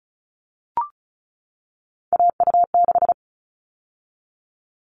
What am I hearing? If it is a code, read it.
Morse code, heard as a keyed tone, about 34 wpm, 700 Hz. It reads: AU6
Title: All Cut Numbers - 3-Digit 35wpm